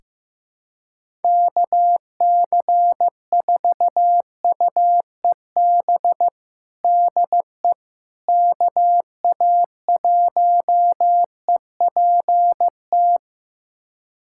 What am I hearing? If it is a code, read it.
Morse code, heard as a keyed tone, about 16 wpm, 700 Hz. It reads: KC4UEB DE KA1EPT
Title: USA Callsign DE Callsign - 6 Characters - 15wpm